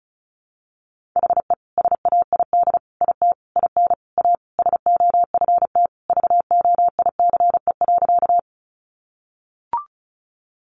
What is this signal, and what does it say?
Morse code, heard as a keyed tone, about 34 wpm, 700 Hz. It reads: HE SAID IT IN A SOFT VOICE.
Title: Sentences from Top 1500 Words 35wpm